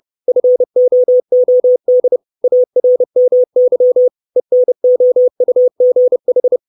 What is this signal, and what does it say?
Morse code, heard as a keyed tone, about 30 wpm, 500 Hz. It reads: FOOD ARMY ENOUGH
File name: 632 Words 30WPM